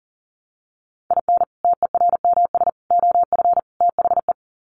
Morse code, heard as a keyed tone, about 40 wpm, 700 Hz. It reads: IN TERMS OF THE
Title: Sets of 4 Words 40wpm